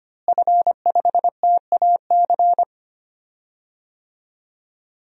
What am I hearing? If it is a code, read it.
Morse code, heard as a keyed tone, about 25 wpm, 700 Hz. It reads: F5TAC
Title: DXCC-100 Call Signs - 5 Characters - Repeated 2x - 25wpm